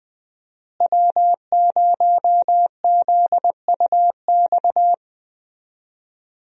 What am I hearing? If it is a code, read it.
Morse code, heard as a keyed tone, about 21 wpm, 700 Hz. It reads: W0ZUX
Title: USA Call Signs - 5 Characters - Repeated 2x - 20wpm